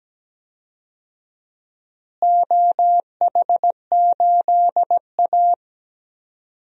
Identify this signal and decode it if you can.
Morse code, heard as a keyed tone, about 17 wpm, 700 Hz. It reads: OH8A
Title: DXCC-100 Call Signs - 4 Characters - Repeated 2x - 17wpm